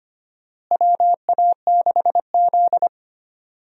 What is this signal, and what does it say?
Morse code, heard as a keyed tone, about 25 wpm, 700 Hz. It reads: WA6Z
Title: USA Call Signs - 4 Characters - Repeated 3x - 25wpm